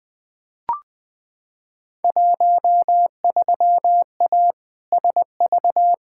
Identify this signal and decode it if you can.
Morse code, heard as a keyed tone, about 20 wpm, 700 Hz. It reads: 13A SV
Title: Field Day Exchanges Encore - 20wpm